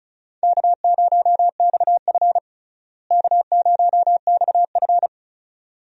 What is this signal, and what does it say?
Morse code, heard as a keyed tone, about 36 wpm, 700 Hz. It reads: K0XF K0XF
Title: USA Call Signs - 4 Characters - Repeated 2x - 35wpm